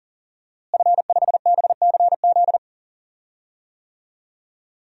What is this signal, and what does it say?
Morse code, heard as a keyed tone, about 40 wpm, 700 Hz. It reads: F5BCZ
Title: DXCC-100 Call Signs - 5 Characters - Repeated 3x - 40wpm